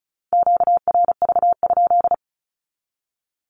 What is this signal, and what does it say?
Morse code, heard as a keyed tone, about 34 wpm, 700 Hz. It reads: QRV?
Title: Q-Codes 35wpm